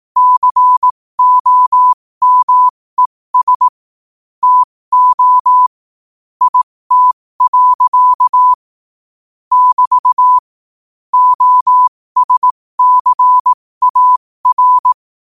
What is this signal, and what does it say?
Morse code, heard as a keyed tone, about 18 wpm, spaced as for 15 wpm, 1000 Hz. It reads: COMES TO IT. <BT> OSCAR
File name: qotdfiles / 15wpm